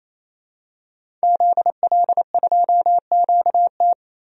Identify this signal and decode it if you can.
Morse code, heard as a keyed tone, about 27 wpm, 700 Hz. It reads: ZL2QT
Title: DXCC-100 Call Signs - 5 Characters - Repeated 3x - 28wpm